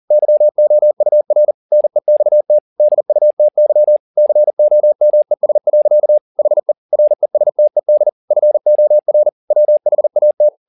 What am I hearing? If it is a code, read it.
Morse code, heard as a keyed tone, about 40 wpm, 600 Hz. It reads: YOUR NEXT DUTY COMES. HE RESTED FOR WHAT
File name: the old man and the sea 2 40wpm